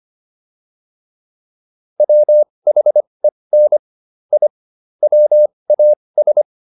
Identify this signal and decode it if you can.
Morse code, heard as a keyed tone, about 25 wpm, spaced as for 20 wpm, 600 Hz. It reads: WHEN I WAS